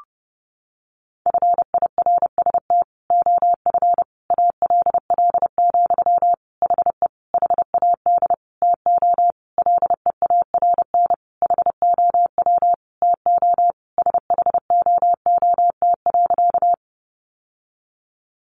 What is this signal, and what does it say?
Morse code, heard as a keyed tone, about 30 wpm, 700 Hz. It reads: FIRST OF ALL, HE HAD TO LEARN HOW TO SHOOT.
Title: Sentences from Top 1200 Words 30wpm